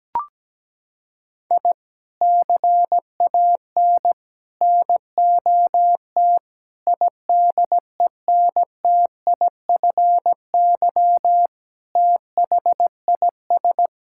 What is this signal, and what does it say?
Morse code, heard as a keyed tone, about 17 wpm, 700 Hz. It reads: I CAN NOT IDENTIFY THIS
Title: Sentences from Top 1000 Words - Review Part 6 - 17wpm